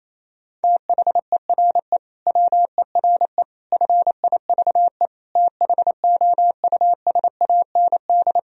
Morse code, heard as a keyed tone, about 27 wpm, 700 Hz. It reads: THERE WERE FIVE THOUSAND
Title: Sentences from Top 1000 Words - Review Part 3 - 28wpm